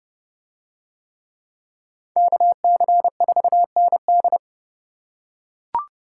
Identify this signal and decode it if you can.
Morse code, heard as a keyed tone, about 29 wpm, 700 Hz. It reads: KC4ND